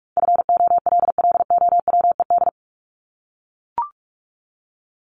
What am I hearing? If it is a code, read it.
Morse code, heard as a keyed tone, about 44 wpm, 700 Hz. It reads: FOLLOWED
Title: ED Suffix - Top 100 Words with Base Words 45wpm